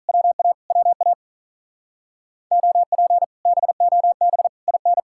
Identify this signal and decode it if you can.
Morse code, heard as a keyed tone, about 45 wpm, 700 Hz. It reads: WA WA OP BOB IN